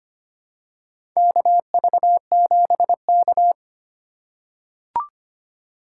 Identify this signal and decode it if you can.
Morse code, heard as a keyed tone, about 25 wpm, 700 Hz. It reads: KV7K
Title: USA Call Signs - 4 Characters - 3x Character Spacing - 25wpm